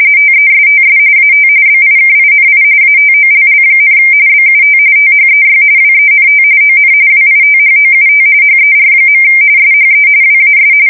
RTTY, about 45.45 baud. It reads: TESTS AT BIKINI ATOLL, BUT THEY WENT ON WITHOUT HER. SHE WAS
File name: fsk170 / battleshipnj